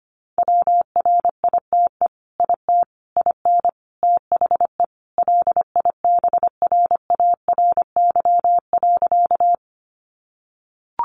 Morse code, heard as a keyed tone, about 25 wpm, 700 Hz. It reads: WRITE IT IN THE LIBRARY.
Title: Sentences from Top 600 Words 25wpm